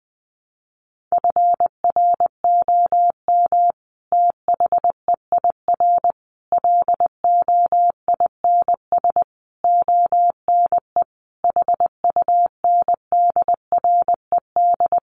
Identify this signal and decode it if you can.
Morse code, heard as a keyed tone, about 20 wpm, 700 Hz. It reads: FROM THEIR LOINS ONE HUNDRED
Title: A Sound Of Thunder 20wpm